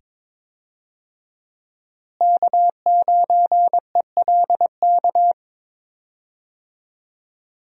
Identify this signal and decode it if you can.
Morse code, heard as a keyed tone, about 22 wpm, 700 Hz. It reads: K9ELK
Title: USA Call Signs - 5 Characters - Repeated 2x - 22wpm